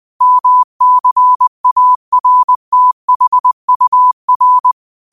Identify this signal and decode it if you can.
Morse code, heard as a keyed tone, about 20 wpm, 1000 Hz. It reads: MCARTHUR
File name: qotdfiles / 20wpm